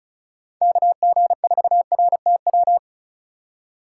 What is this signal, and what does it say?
Morse code, heard as a keyed tone, about 36 wpm, 700 Hz. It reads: KG4RTW
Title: USA Call Signs - 6 Characters - 3x Character Spacing - 35wpm